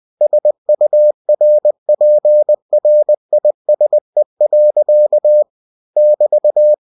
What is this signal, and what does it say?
Morse code, heard as a keyed tone, about 20 wpm, 600 Hz. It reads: SURPRISE. <BT>
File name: cw-books / winnie-the-pooh / winnie-20wpm-page-0046